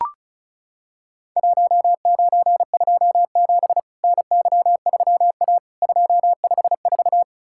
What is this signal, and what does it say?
Morse code, heard as a keyed tone, about 36 wpm, 700 Hz. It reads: 1927 NY3A 254
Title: Worked All Europe DX Contest - QTC Exchange - 35wpm